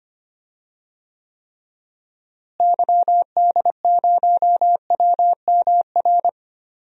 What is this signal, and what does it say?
Morse code, heard as a keyed tone, about 25 wpm, 700 Hz. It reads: YD0WMR
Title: DXCC-100 Call Signs - 6 Characters - Repeated 3x - 25wpm